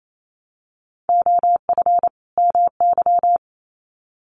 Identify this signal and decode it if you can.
Morse code, heard as a keyed tone, about 28 wpm, 700 Hz. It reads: OF MY
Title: Sets of 2 Words 28wpm